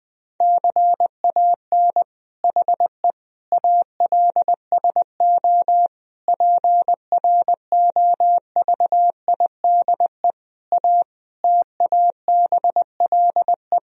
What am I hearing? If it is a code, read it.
Morse code, heard as a keyed tone, about 20 wpm, 700 Hz. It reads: CAN HE ALSO PROVIDE A TABLE
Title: Sentences from Top 1000 Words - Review Part 3 - 20wpm